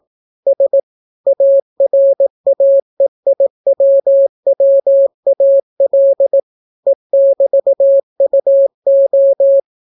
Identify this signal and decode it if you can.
Morse code, heard as a keyed tone, about 18 wpm, 550 Hz. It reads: S ARAEIWWAL E<BT>UO